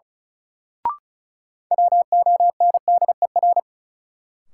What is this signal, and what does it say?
Morse code, heard as a keyed tone, about 36 wpm, 700 Hz. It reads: WONDER
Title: Sentences from Top 500 Words - New Word Drill - 35wpm